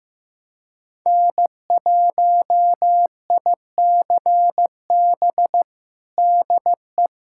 Morse code, heard as a keyed tone, about 15 wpm, 700 Hz. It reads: N1ICB DE
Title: USA Callsign DE Callsign - 5 Characters - 15wpm